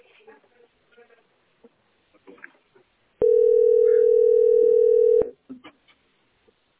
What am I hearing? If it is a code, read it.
Ringback tone (North America), 440 + 480 Hz.